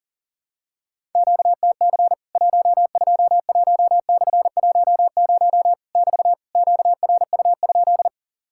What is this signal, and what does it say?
Morse code, heard as a keyed tone, about 40 wpm, 700 Hz. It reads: QTC 121/10 <BT> QRU?
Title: Worked All Europe DX Contest - QTC Exchange - 40wpm